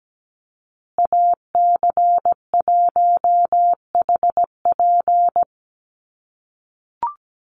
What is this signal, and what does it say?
Morse code, heard as a keyed tone, about 17 wpm, 700 Hz. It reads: AC1HP
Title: USA Call Signs Repeated 2x 17wpm